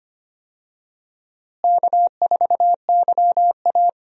Morse code, heard as a keyed tone, about 25 wpm, 700 Hz. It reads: K4YA
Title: USA Call Signs - 4 Characters - Repeated 2x - 25wpm